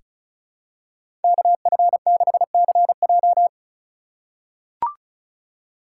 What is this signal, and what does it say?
Morse code, heard as a keyed tone, about 34 wpm, 700 Hz. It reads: KF6CJ
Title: USA Call Signs - 5 Characters - 4x Character Spacing - 35wpm